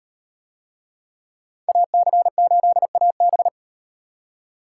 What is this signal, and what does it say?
Morse code, heard as a keyed tone, about 36 wpm, 700 Hz. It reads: AC8AB